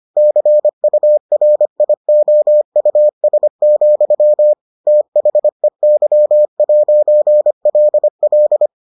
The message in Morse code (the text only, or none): CURIOUS, THEY'LL